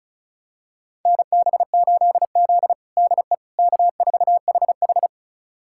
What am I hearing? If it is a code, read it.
Morse code, heard as a keyed tone, about 36 wpm, 700 Hz. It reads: NB8Z DE K4HH
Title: USA Callsign DE Callsign - 4 Characters - 35wpm